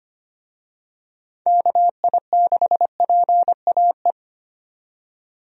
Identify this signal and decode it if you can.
Morse code, heard as a keyed tone, about 25 wpm, 700 Hz. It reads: KI6PAE
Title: USA Call Signs - 6 Characters - 25wpm